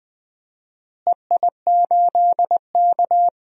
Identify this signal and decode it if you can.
Morse code, heard as a keyed tone, about 20 wpm, 700 Hz. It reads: EI8K
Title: DXCC-100 Call Signs - 4 Characters - Repeated 3x - 20wpm